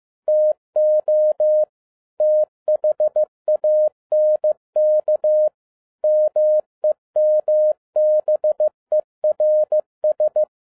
Morse code, heard as a keyed tone, about 15 wpm, 610 Hz. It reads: TO THANK MEMBERS
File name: 2012-01-mp3 / n17-SUMMARY OF FEA NET IN 2011-15wpm